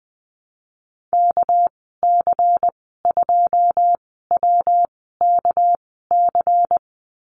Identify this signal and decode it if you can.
Morse code, heard as a keyed tone, about 20 wpm, spaced as for 14 wpm, 700 Hz. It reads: KC2WKC
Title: 231 Lesson 37g - Letter x - Call Signs 20wpm